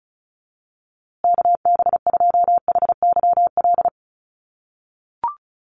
Morse code, heard as a keyed tone, about 34 wpm, 700 Hz. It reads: KB2HYL
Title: USA Call Signs Repeated 2x 35wpm